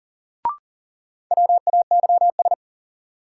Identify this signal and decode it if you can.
Morse code, heard as a keyed tone, about 40 wpm, 700 Hz. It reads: WAYS
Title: Sets of 5 Words Encore - New Word Drill - 40wpm